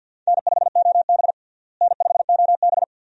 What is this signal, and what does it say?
Morse code, heard as a keyed tone, about 52 wpm, 700 Hz. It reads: N5OB N5OB
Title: USA Call Signs - 4 Characters - Repeated 2x - 50wpm